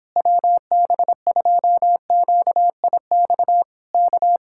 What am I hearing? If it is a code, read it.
Morse code, heard as a keyed tone, about 26 wpm, 700 Hz. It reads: WB2QIX K
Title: Speed-Racing Common QSO Phrases - 22wpm